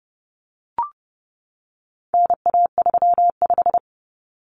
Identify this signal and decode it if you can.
Morse code, heard as a keyed tone, about 30 wpm, 700 Hz. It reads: NA35
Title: Common Cut Numbers - 4-Digit 30wpm